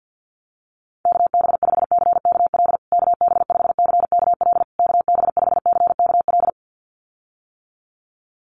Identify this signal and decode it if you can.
Morse code, heard as a keyed tone, about 49 wpm, 700 Hz. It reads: KB5CKL KB5CKL KB5CKL
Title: USA Call Signs Repeated 3x 50wpm